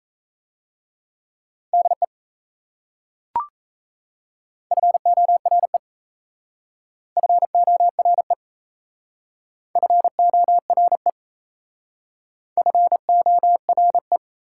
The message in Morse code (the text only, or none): DE FORE FORE FORE FORE